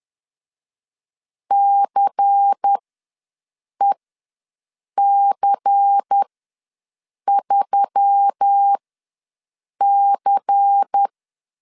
Morse code, heard as a keyed tone, about 10 wpm, spaced as for 6 wpm, 800 Hz. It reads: CEC3C
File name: LEZIONI / 5 LBVC / 5 LBVC ws 10wpm 3733